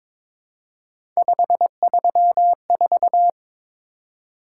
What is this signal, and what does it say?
Morse code, heard as a keyed tone, about 23 wpm, 700 Hz. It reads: 534
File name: practice / 22wpm / Three Letter-Number Combinations - 22wpm